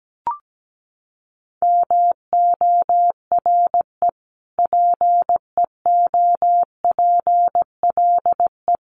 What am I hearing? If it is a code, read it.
Morse code, heard as a keyed tone, about 17 wpm, 700 Hz. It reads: MORE PEOPLE
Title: Sentences from Top 1500 Words 17wpm